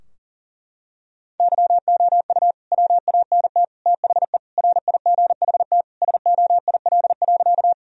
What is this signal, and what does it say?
Morse code, heard as a keyed tone, about 40 wpm, 700 Hz. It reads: YOU WANT THE RIGHT SOIL.
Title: Sentences from Top 1000 Words - Review Part 4 - 40wpm